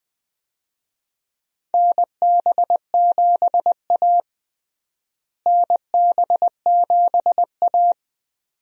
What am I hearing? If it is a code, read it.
Morse code, heard as a keyed tone, about 20 wpm, 700 Hz. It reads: NB7A NB7A
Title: USA Call Signs - 4 Characters - Repeated 2x - 20wpm